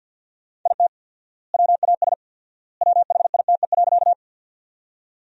Morse code, heard as a keyed tone, about 53 wpm, 700 Hz. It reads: IT WAS WHITE.
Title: 4x Word-Spacing - Sentences from Top 300 Words - 50wpm